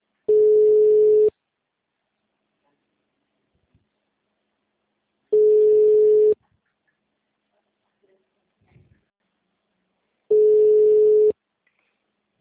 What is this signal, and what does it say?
Ringback tone (Europe, 425 Hz).
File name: out- -2003- - - .23162